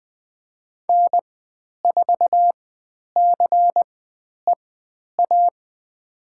Morse code, heard as a keyed tone, about 20 wpm, spaced as for 10 wpm, 700 Hz. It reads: N4CEA
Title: 102 Lesson 17g - Letter u - Call Signs 20wpm